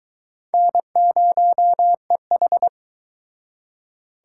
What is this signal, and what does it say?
Morse code, heard as a keyed tone, about 24 wpm, 700 Hz. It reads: N0EH